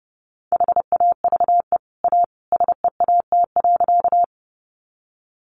Morse code, heard as a keyed tone, about 30 wpm, 700 Hz. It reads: HAVE A SEAT.
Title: Sentences from Top 1500 Words 30wpm